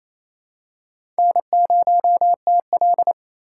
Morse code, heard as a keyed tone, about 28 wpm, 700 Hz. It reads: N0TL
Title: USA Call Signs - 4 Characters - 4x Character Spacing - 28wpm